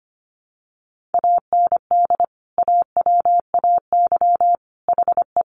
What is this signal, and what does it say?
Morse code, heard as a keyed tone, about 25 wpm, 700 Hz. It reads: AND AWAY HE
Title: Sentences from Top 600 Words 25wpm